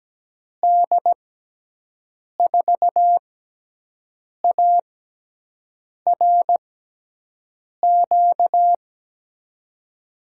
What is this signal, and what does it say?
Morse code, heard as a keyed tone, about 17 wpm, spaced as for 6 wpm, 700 Hz. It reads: D4ARQ
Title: USA Call Signs - 6 Characters - 6x Character Spacing - 17wpm